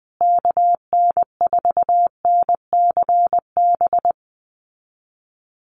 Morse code, heard as a keyed tone, about 20 wpm, 700 Hz. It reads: KN4NCB
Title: USA Call Signs Repeated 3x 20wpm